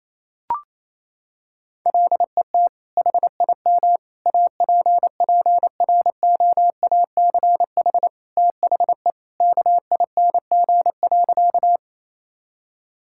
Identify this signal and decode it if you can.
Morse code, heard as a keyed tone, about 28 wpm, 700 Hz. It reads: LET HIM APPROACH THE KING.
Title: Sentences from Top 1000 Words - Review Part 6 - 28wpm